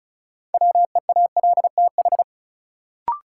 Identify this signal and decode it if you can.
Morse code, heard as a keyed tone, about 36 wpm, 700 Hz. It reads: WEALTH
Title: Sentences from Top 1200 Words - New Word Drill Alt - 35wpm